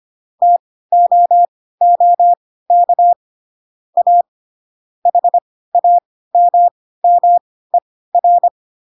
Morse code, heard as a keyed tone, about 25 wpm, spaced as for 16 wpm, 700 Hz. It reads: TOOK A HAMMER
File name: Pinocchio / Pinocchio ch4 10 25 ik1hge